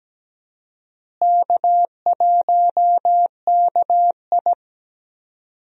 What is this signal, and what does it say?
Morse code, heard as a keyed tone, about 17 wpm, 700 Hz. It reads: K1KI
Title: USA Call Signs - 4 Characters - Repeated 2x - 17wpm